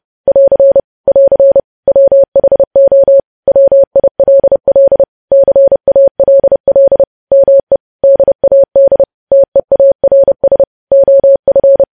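Morse code, heard as a keyed tone, about 30 wpm, 550 Hz. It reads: <AR> <AR> WHO WILL CALL ME DAD TEARS OF
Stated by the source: News headlines sent in Morse code at 30.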